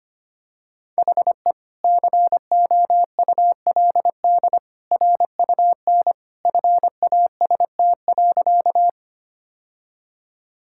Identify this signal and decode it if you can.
Morse code, heard as a keyed tone, about 25 wpm, 700 Hz. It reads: HE COULD RUN FAST.
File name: practice / 25wpm / Sentences with 2-Word Phrases - 25wpm